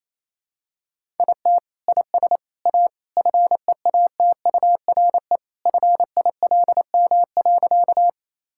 Morse code, heard as a keyed tone, about 28 wpm, 700 Hz. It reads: IT IS A FEATURE FILM.
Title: Sentences from Top 1000 Words - Review Part 5 - 28wpm